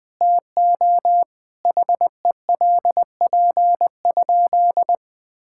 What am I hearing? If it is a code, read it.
Morse code, heard as a keyed tone, about 20 wpm, 700 Hz. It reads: TO HELP?
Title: Sentences with 4-Word Phrases - Spoken Beginning - 20wpm